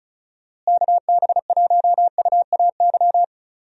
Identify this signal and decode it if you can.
Morse code, heard as a keyed tone, about 34 wpm, 700 Hz. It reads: KB1UAY
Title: USA Call Signs - 6 Characters - Repeated 2x - 35wpm